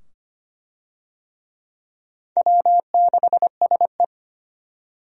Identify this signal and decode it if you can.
Morse code, heard as a keyed tone, about 25 wpm, 700 Hz. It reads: W6SE
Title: USA Call Signs - 4 Characters - 25wpm